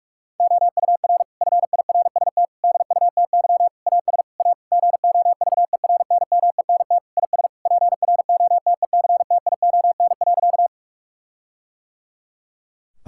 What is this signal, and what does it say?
Morse code, heard as a keyed tone, about 41 wpm, 700 Hz. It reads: OUR FIRST DUTY AS A GOVERNMENT IS PROTECTION.
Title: Sentences from Top 1600 Words 45wpm